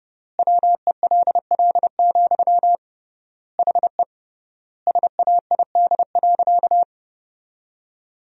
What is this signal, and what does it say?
Morse code, heard as a keyed tone, about 29 wpm, 700 Hz. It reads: WELL, HE SAID.
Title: 3x Word-Spacing - Sentences from Top 500 Words - 30wpm